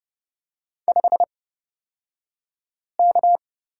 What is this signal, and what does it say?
Morse code, heard as a keyed tone, about 30 wpm, spaced as for 5 wpm, 700 Hz. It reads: 5K
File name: Lesson 36d - BT - Two Characters 30wpm